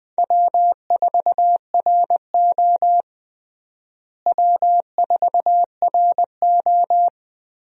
Morse code, heard as a keyed tone, about 20 wpm, 700 Hz. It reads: W4RO W4RO
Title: USA Call Signs - 4 Characters - Repeated 2x - 20wpm